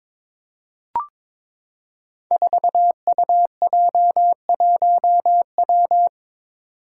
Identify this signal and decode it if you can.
Morse code, heard as a keyed tone, about 22 wpm, 700 Hz. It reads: 4UJ1W
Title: Five Challenging Letter-Number Combinations 22wpm